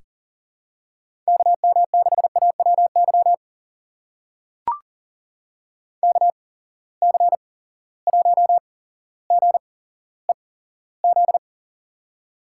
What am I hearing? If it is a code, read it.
Morse code, heard as a keyed tone, about 40 wpm, spaced as for 17 wpm, 700 Hz. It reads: KM6AWY KC1GEZ
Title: USA Call Signs - 6 Characters - 8x Character Spacing - 40wpm